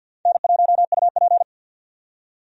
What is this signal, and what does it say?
Morse code, heard as a keyed tone, about 47 wpm, 700 Hz. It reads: N1UP
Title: USA Call Signs - 4 Characters - 50wpm